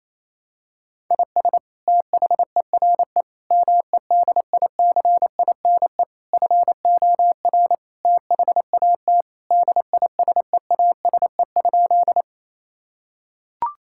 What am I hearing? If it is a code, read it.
Morse code, heard as a keyed tone, about 27 wpm, 700 Hz. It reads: IS THERE MEDICINE FOR THAT DISEASE?
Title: Sentences from Top 1000 Words - Review Part 6 - 28wpm